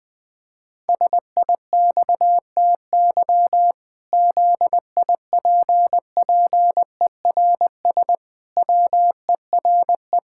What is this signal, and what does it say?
Morse code, heard as a keyed tone, about 20 wpm, 700 Hz. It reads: SIXTY ZIPPERS WERE
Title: Pangram Challenge v2 - 20wpm